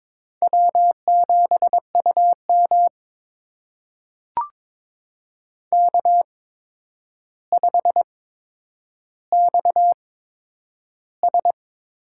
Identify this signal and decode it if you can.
Morse code, heard as a keyed tone, about 22 wpm, spaced as for 9 wpm, 700 Hz. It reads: W7UM K5XS
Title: USA Call Signs - 4 Characters - 8x Character Spacing - 22wpm